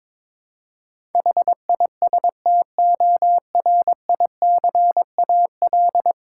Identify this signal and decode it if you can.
Morse code, heard as a keyed tone, about 22 wpm, 700 Hz. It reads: HISTORICAL
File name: Difficult Words with Hints - 22wpm